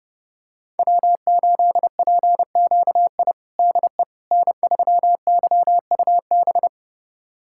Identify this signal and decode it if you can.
Morse code, heard as a keyed tone, about 31 wpm, 700 Hz. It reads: W8PQI DE N3YUB
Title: USA Callsign DE Callsign - 5 Characters - 30wpm